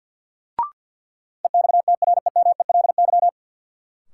Morse code, heard as a keyed tone, about 52 wpm, 700 Hz. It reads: EXTREMELY
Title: Sentences from Top 1000 Words - Review Part 7 - New Word Drill - 50wpm